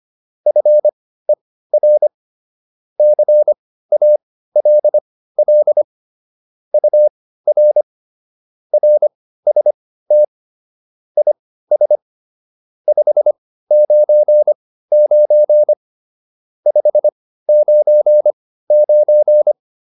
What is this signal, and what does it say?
Morse code, heard as a keyed tone, about 25 wpm, spaced as for 15 wpm, 600 Hz. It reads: FER CALL UR RST IS 599 599